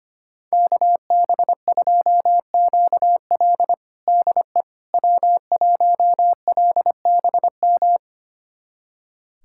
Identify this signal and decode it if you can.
Morse code, heard as a keyed tone, about 25 wpm, 700 Hz. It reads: KB2QL DE W1LBM
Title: USA Callsign DE Callsign - 5 Characters - 25wpm